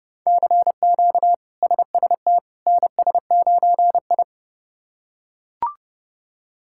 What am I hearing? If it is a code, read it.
Morse code, heard as a keyed tone, about 30 wpm, 700 Hz. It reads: CQ SST NS9I
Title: K1USN SST QSO Practice - 30wpm